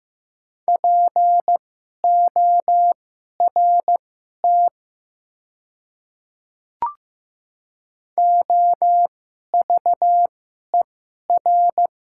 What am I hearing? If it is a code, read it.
Morse code, heard as a keyed tone, about 15 wpm, spaced as for 11 wpm, 700 Hz. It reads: PORT OVER
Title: 2x Character Spacing - Top 300 Words - 15wpm